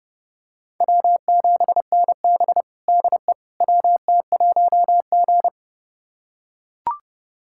Morse code, heard as a keyed tone, about 30 wpm, 700 Hz. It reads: W7NB DE WT1G
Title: USA Callsign DE Callsign - 4 Characters - 30wpm